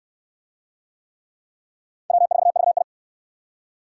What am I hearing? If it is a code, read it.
Morse code, heard as a keyed tone, about 60 wpm, 700 Hz. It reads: W4VI